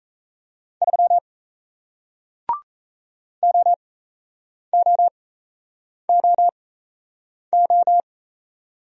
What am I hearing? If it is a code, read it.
Morse code, heard as a keyed tone, about 38 wpm, spaced as for 8 wpm, 700 Hz. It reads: 3 OOOO